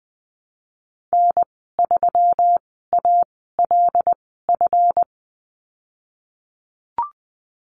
Morse code, heard as a keyed tone, about 20 wpm, spaced as for 15 wpm, 700 Hz. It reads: N3ALF
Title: 147 Lesson 24g - Letter y - Call Signs 20wpm